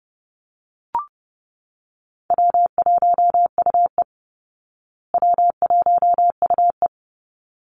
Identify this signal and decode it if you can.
Morse code, heard as a keyed tone, about 29 wpm, 700 Hz. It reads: W1UE W1UE
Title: Easy Call Signs Repeated 2x - 30wpm